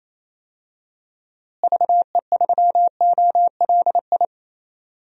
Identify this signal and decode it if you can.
Morse code, heard as a keyed tone, about 28 wpm, 700 Hz. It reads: VE3OLI
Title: DXCC-100 Call Signs - 6 Characters - Repeated 2x - 28wpm